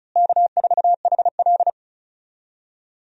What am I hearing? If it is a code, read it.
Morse code, heard as a keyed tone, about 36 wpm, 700 Hz. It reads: K4HL
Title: USA Call Signs - 4 Characters - Repeated 3x - 35wpm